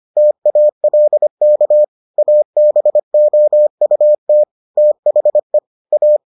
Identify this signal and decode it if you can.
Morse code, heard as a keyed tone, about 25 wpm, 600 Hz. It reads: TALK ABOUT THE A